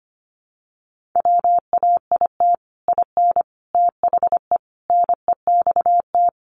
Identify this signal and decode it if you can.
Morse code, heard as a keyed tone, about 25 wpm, 700 Hz. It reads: WAIT IN THE NEXT
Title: Sentences from Top 1500 Words 25wpm